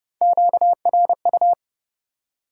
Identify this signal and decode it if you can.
Morse code, heard as a keyed tone, about 30 wpm, 700 Hz. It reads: QRU
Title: Top 90 QSO Elements 30wpm